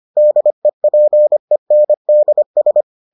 Morse code, heard as a keyed tone, about 25 wpm, 600 Hz. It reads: DEPENDS